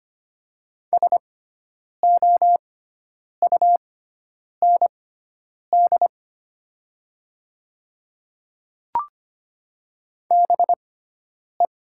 Morse code, heard as a keyed tone, about 25 wpm, spaced as for 9 wpm, 700 Hz. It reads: SOUND BE
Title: 6x Character Spacing - Top 100 Words - 25wpm